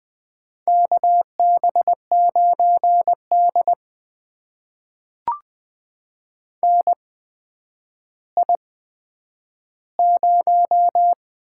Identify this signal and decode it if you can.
Morse code, heard as a keyed tone, about 20 wpm, spaced as for 10 wpm, 700 Hz. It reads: KB9D NI0
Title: USA Call Signs - 4 Characters - 8x Character Spacing - 20wpm